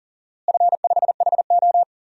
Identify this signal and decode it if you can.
Morse code, heard as a keyed tone, about 38 wpm, 700 Hz. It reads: F5HO
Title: DXCC-100 Call Signs - 4 Characters - Repeated 3x - 40wpm